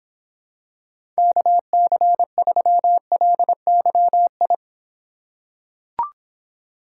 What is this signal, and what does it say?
Morse code, heard as a keyed tone, about 26 wpm, 700 Hz. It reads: KC3LYI